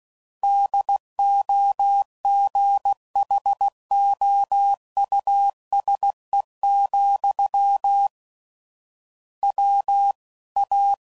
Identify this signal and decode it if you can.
Morse code, heard as a keyed tone, about 16 wpm, 800 Hz. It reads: DOGHOUSE, W A
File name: Suffixes / General / Houses